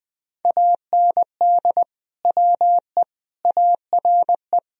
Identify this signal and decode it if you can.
Morse code, heard as a keyed tone, about 21 wpm, 700 Hz. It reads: AND WE ARE
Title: Sentences from Top 1000 Words - Review Part 1 - 20wpm